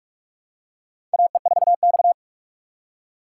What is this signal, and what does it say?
Morse code, heard as a keyed tone, about 44 wpm, 690 Hz. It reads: AE4X
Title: USA Call Signs - 4 Characters - 4x Character Spacing - 45wpm